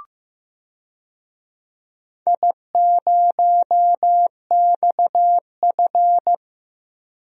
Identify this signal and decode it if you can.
Morse code, heard as a keyed tone, about 15 wpm, 700 Hz. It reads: I0XF
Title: DXCC-100 Call Signs - 4 Characters - Repeated 2x - 15wpm